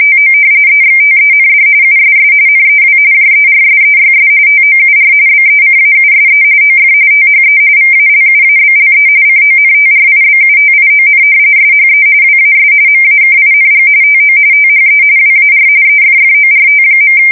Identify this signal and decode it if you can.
RTTY, about 45.45 baud. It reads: TO EXTEND USS NEW JERSEYS TOUR BECAUSE OF THE VALUABLE CONTRIBUTION SHE WAS MAKING TO THE WAR EFFORT.
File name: fsk170 / battleshipnj